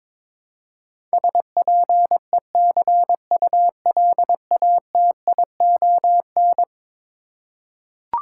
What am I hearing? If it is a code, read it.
Morse code, heard as a keyed tone, about 21 wpm, 700 Hz. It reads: SPECULATION
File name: practice / 22wpm / Eleven Letter Words - 22wpm